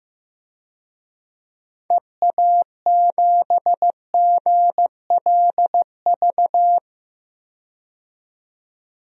Morse code, heard as a keyed tone, about 15 wpm, 700 Hz. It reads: EA7GLV
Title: DXCC-100 Call Signs - 6 Characters - 15wpm